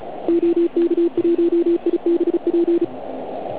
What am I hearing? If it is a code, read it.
Morse code, heard as a keyed tone, about 36 wpm, 340 Hz. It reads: OK1IBP
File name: Archiv / CUC129 / CUC129 pileup